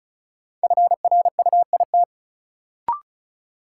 Morse code, heard as a keyed tone, about 36 wpm, 700 Hz. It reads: FRUIT
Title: Sentences from Top 1300 Words - New Word Drill - 35wpm